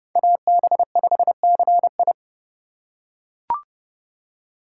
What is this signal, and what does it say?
Morse code, heard as a keyed tone, about 31 wpm, 700 Hz. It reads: AB5CI